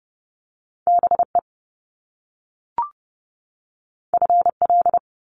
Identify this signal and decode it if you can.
Morse code, heard as a keyed tone, about 30 wpm, 700 Hz. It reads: BE FL
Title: Top Two Letter Combinations in Words 30wpm